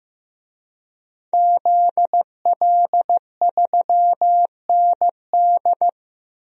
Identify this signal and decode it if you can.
Morse code, heard as a keyed tone, about 15 wpm, 700 Hz. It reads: ZL3ND
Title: DXCC-100 Call Signs - 5 Characters - Repeated 3x - 15wpm